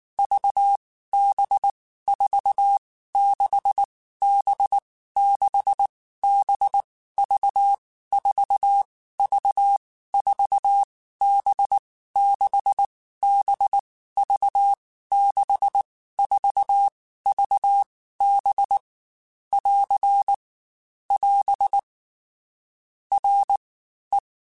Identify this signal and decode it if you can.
Morse code, heard as a keyed tone, about 19 wpm, 800 Hz. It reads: V B 4 6 B 6 B V 4 V 4 B 6 B V 6 4 V B <AR> <AS> R E
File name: Basic Letter Groups By7Groups / VB46 / VB46@18